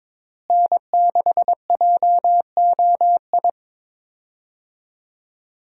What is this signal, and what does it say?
Morse code, heard as a keyed tone, about 22 wpm, 700 Hz. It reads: N6JOI
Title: USA Call Signs - 5 Characters - Repeated 2x - 22wpm